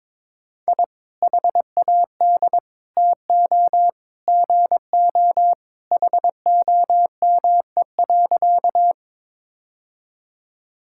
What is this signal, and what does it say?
Morse code, heard as a keyed tone, about 21 wpm, 700 Hz. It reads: I HAD TO GO HOME.
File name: practice / 22wpm / Sentences with 3-Word Phrases - 22wpm